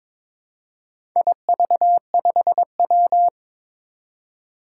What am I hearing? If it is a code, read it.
Morse code, heard as a keyed tone, about 22 wpm, 700 Hz. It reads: IV5W
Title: Four Challenging Letter-Number Combinations 22wpm